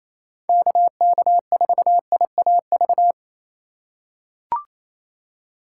Morse code, heard as a keyed tone, about 28 wpm, 700 Hz. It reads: KK4IAV
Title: USA Call Signs - 6 Characters - 4x Character Spacing - 28wpm